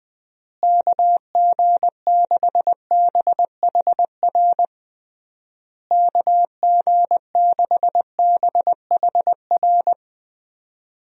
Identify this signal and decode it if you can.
Morse code, heard as a keyed tone, about 21 wpm, 700 Hz. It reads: KG6BHR KG6BHR
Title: USA Call Signs - 6 Characters - Repeated 3x - 20wpm